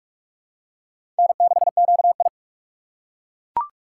Morse code, heard as a keyed tone, about 41 wpm, 700 Hz. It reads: N6QI
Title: USA Call Signs - 4 Characters - 6x Character Spacing - 45wpm